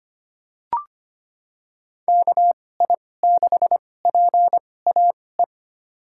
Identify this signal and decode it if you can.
Morse code, heard as a keyed tone, about 25 wpm, spaced as for 18 wpm, 700 Hz. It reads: KI6PAE
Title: USA Call Signs - 6 Characters - 2x Character Spacing - 25wpm